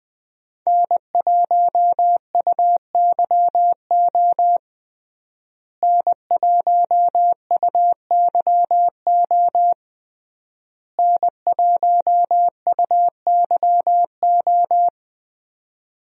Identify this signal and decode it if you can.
Morse code, heard as a keyed tone, about 21 wpm, 700 Hz. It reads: N1UYO N1UYO N1UYO
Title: USA Call Signs - 5 Characters - Repeated 3x - 20wpm